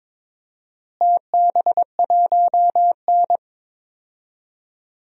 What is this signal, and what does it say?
Morse code, heard as a keyed tone, about 22 wpm, 700 Hz. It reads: TB1N
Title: DXCC-100 Call Signs - 4 Characters - Repeated 2x - 22wpm